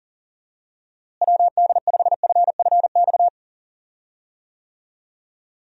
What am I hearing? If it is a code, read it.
Morse code, heard as a keyed tone, about 40 wpm, 700 Hz. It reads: WD5FFX
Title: USA Call Signs - 6 Characters - Repeated 3x - 40wpm